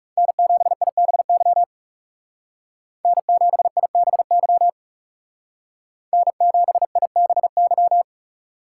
Morse code, heard as a keyed tone, about 36 wpm, 700 Hz. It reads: N7IBY N7IBY N7IBY
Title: Speed Racing - USA 1x3 Call Signs 30wpm